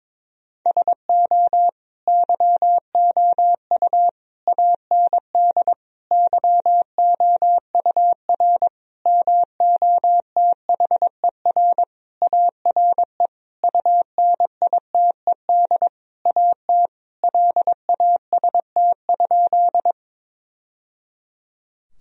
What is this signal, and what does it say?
Morse code, heard as a keyed tone, about 22 wpm, 700 Hz. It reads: SO YOU AND YOUR MOTHER ARE UNITED AT LAST?
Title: Sentences from Top 1000 Words - Review Part 4 - 22wpm